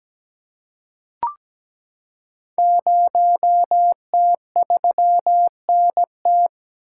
Morse code, heard as a keyed tone, about 17 wpm, 700 Hz. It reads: 0T3NT
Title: Code Groups - 5 Letter:Numbers - 17wpm